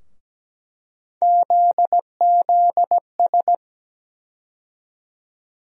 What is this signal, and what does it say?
Morse code, heard as a keyed tone, about 17 wpm, 700 Hz. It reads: ZZS
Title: USA Call Sign Postfixes - 3 Letters 17wpm